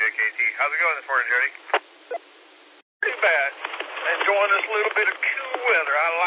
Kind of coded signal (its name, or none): Emergency Alert System (SAME) tones